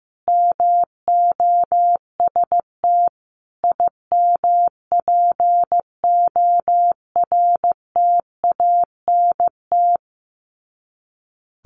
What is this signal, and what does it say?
Morse code, heard as a keyed tone, about 15 wpm, 700 Hz. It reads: MOST IMPORTANT
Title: Sets of 4 Words 15wpm